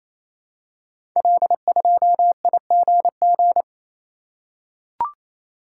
Morse code, heard as a keyed tone, about 28 wpm, 700 Hz. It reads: L2IGG
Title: Code Groups - 5 Letter:Numbers - 28wpm